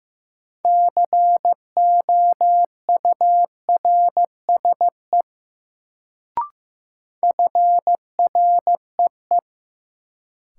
Morse code, heard as a keyed tone, about 15 wpm, 700 Hz. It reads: COURSE FREE
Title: Sentences from Top 500 Words - New Word Drill - 15wpm